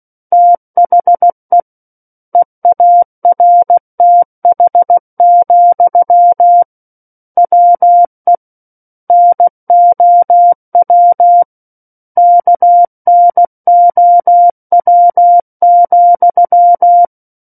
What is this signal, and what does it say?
Morse code, heard as a keyed tone, about 16 wpm, 700 Hz. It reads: THE EARTH, WE NOW KNOW,
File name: Book One / 07-Chapter 7 - 16 WPM